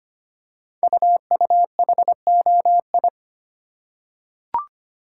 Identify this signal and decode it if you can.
Morse code, heard as a keyed tone, about 25 wpm, 700 Hz. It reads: UUHOI
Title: Code Groups - 5 Letters - 25wpm